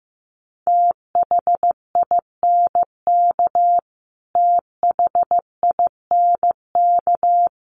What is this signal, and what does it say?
Morse code, heard as a keyed tone, about 15 wpm, 700 Hz. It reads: THINK THINK
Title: Top 500 Words Repeated 15wpm